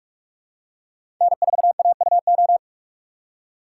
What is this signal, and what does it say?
Morse code, heard as a keyed tone, about 47 wpm, 700 Hz. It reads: N4AUO
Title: USA Call Signs - 5 Characters - 6x Character Spacing - 45wpm